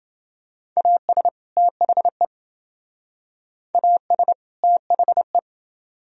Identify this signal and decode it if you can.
Morse code, heard as a keyed tone, about 28 wpm, 700 Hz. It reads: AS THE AS THE